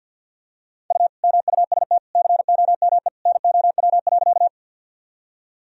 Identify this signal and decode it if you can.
Morse code, heard as a keyed tone, about 51 wpm, 700 Hz. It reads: U MUST COME NOW.
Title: Sentences from Top 200 Words and Abbreviations 50wpm